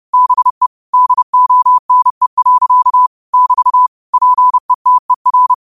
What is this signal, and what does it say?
Morse code, heard as a keyed tone, about 30 wpm, 1000 Hz. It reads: BE DONE. <BT> PETER
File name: qotdfiles / 30wpm